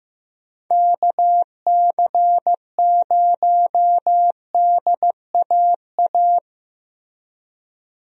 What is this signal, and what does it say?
Morse code, heard as a keyed tone, about 15 wpm, 700 Hz. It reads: KC0DAA
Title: USA Call Signs - 6 Characters - Repeated 3x - 15wpm